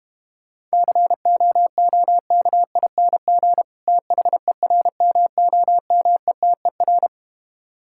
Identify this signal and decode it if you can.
Morse code, heard as a keyed tone, about 32 wpm, 700 Hz. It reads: COOKING THERMOMETER